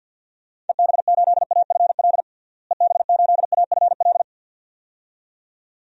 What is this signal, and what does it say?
Morse code, heard as a keyed tone, about 53 wpm, 700 Hz. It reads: EB8AFL EB8AFL
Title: DXCC-100 Call Signs - 6 Characters - Repeated 2x - 50wpm